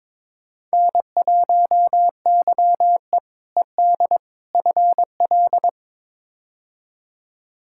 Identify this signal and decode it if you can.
Morse code, heard as a keyed tone, about 22 wpm, 700 Hz. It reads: N1YE ED FL
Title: K1USN SST QSO Practice - 22wpm